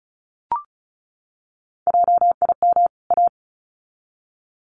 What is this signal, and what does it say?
Morse code, heard as a keyed tone, about 34 wpm, 700 Hz. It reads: JIM A
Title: CWT Exchange 35wpm